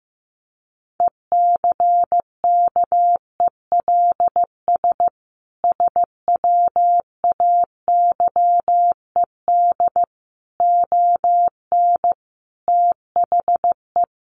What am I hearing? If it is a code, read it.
Morse code, heard as a keyed tone, about 15 wpm, 700 Hz. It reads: ECKELS SWAYED ON THE
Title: A Sound Of Thunder 15wpm